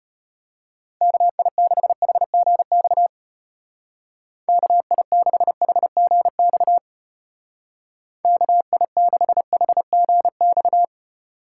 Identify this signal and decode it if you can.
Morse code, heard as a keyed tone, about 34 wpm, 700 Hz. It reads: KI6HGX KI6HGX KI6HGX